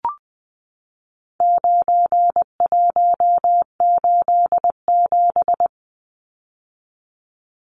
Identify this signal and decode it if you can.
Morse code, heard as a keyed tone, about 20 wpm, 700 Hz. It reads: 9187
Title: Numbers - 4-Digit 20wpm